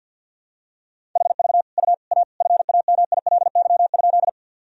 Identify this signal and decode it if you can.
Morse code, heard as a keyed tone, about 48 wpm, 700 Hz. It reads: HV U A FAMILY?
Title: Sentences from Top 400 Words and Abbreviations 50wpm